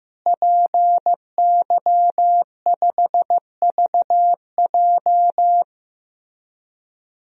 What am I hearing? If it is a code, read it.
Morse code, heard as a keyed tone, about 15 wpm, 700 Hz. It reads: PY5VJ
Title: DXCC-100 Call Signs - 5 Characters - Repeated 3x - 15wpm